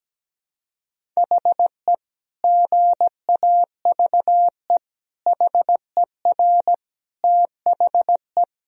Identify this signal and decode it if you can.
Morse code, heard as a keyed tone, about 17 wpm, 700 Hz. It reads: HE GAVE HER THE
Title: Sentences from Top 1000 Words - Review Part 1 - 17wpm